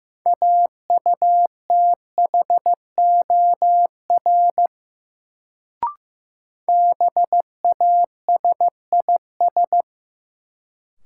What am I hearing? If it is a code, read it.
Morse code, heard as a keyed tone, about 15 wpm, 700 Hz. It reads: AUTHOR BASIS
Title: Sentences from Top 1200 Words - New Word Drill Alt - 15wpm